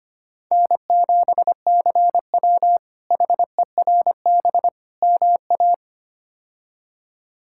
Morse code, heard as a keyed tone, about 25 wpm, 700 Hz. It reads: N7CW HERB MA
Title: K1USN SST QSO Practice - 25wpm